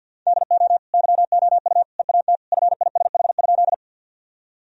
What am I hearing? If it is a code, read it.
Morse code, heard as a keyed tone, about 53 wpm, 700 Hz. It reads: DO YOU EAT FISH?
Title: Sentences from Top 1000 Words - Review Part 1 - 50wpm